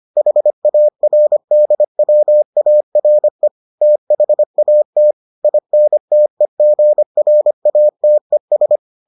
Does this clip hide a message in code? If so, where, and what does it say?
Yes: Morse code from 0.2 to 8.8 s, heard as a keyed tone, about 25 wpm, 600 Hz. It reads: HARDWARE THAT INTEGRATES